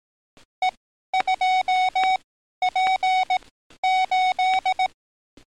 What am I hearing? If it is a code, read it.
Morse code, heard as a keyed tone, about 18 wpm, spaced as for 12 wpm, 740 Hz. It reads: E2P8
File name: training / NAVPERS 11049to11060 / NAVPERS 11053 RA RB